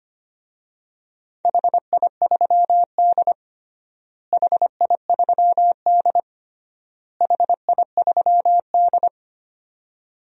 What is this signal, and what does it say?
Morse code, heard as a keyed tone, about 25 wpm, 700 Hz. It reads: HI3D HI3D HI3D
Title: DXCC-100 Call Signs - 4 Characters - Repeated 3x - 25wpm